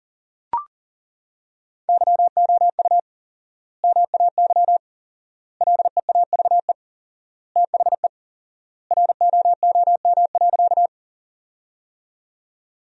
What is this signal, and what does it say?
Morse code, heard as a keyed tone, about 40 wpm, 700 Hz. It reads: YOU MAY LEAVE THE ROOM.
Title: 4x Word-Spacing - Sentences from Top 400 Words - 40wpm